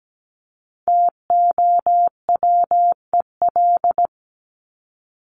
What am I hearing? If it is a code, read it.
Morse code, heard as a keyed tone, about 17 wpm, 700 Hz. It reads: TOWEL
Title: Top 1501-1750 Words 17wpm